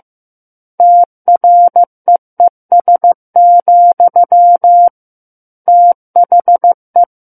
Morse code, heard as a keyed tone, about 15 wpm, 700 Hz. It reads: TREES, THE
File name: Book One / 06-Chapter 6 - 15 WPM